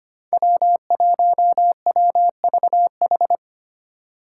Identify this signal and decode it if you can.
Morse code, heard as a keyed tone, about 25 wpm, 700 Hz. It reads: W1WVH
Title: Five Challenging Letter-Number Combinations 25wpm